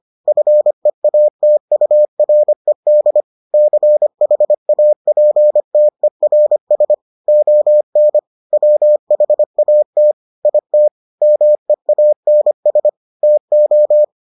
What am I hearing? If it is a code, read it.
Morse code, heard as a keyed tone, about 25 wpm, 600 Hz. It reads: FEATURED CHAPTERS ON WHAT IT MEANS TO